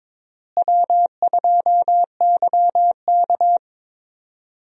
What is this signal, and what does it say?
Morse code, heard as a keyed tone, about 22 wpm, 700 Hz. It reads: W2YK
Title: USA Call Signs - 4 Characters - Repeated 2x - 22wpm